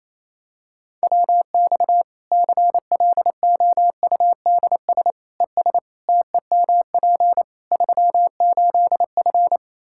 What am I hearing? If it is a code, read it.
Morse code, heard as a keyed tone, about 28 wpm, 700 Hz. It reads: WX CLOUDS ES TEMP 38F
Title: Practice QSO - Exchange 2 - Info - 28wpm